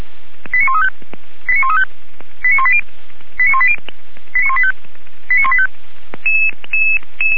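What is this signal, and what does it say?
Five-tone selective call (ZVEI), heard as quick runs of five tones, tones of about 70 ms. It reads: 89135 89135 89180 89180 89185 89185
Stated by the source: ZVEI five-tone call received by SDR, from the Feuerwehr Orlamünde audio server.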